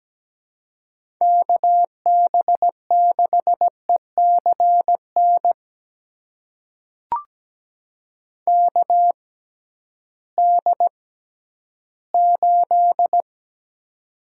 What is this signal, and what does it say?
Morse code, heard as a keyed tone, about 17 wpm, spaced as for 11 wpm, 700 Hz. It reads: KB6ECN KD8
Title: USA Call Signs - 6 Characters - 6x Character Spacing - 17wpm